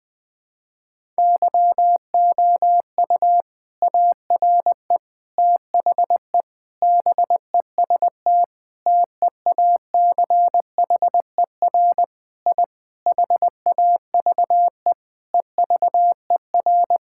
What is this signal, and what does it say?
Morse code, heard as a keyed tone, about 21 wpm, 700 Hz. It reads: YOU ARE THE BEST TEACHER I HAVE EVER
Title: Sentences from Top 1000 Words - Review Part 6 - 20wpm